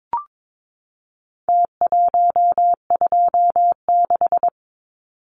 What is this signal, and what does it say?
Morse code, heard as a keyed tone, about 22 wpm, 700 Hz. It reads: T126
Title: Numbers - Contest Style - 4-Digit 22wpm